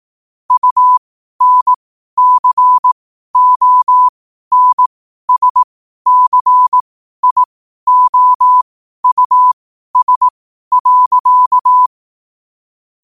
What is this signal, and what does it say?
Morse code, heard as a keyed tone, about 18 wpm, spaced as for 12 wpm, 1000 Hz. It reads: UNCONSCIOUS.
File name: qotdfiles / 12wpm